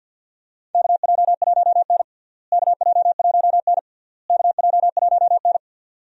Morse code, heard as a keyed tone, about 46 wpm, 700 Hz. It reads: KJ1N KJ1N KJ1N
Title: USA Call Signs - 4 Characters - Repeated 3x - 50wpm